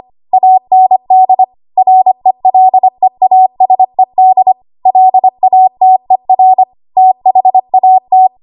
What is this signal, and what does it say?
Morse code, heard as a keyed tone, about 25 wpm, 750 Hz. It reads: AND RELEASED LATER THAT
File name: Morse / 25WPM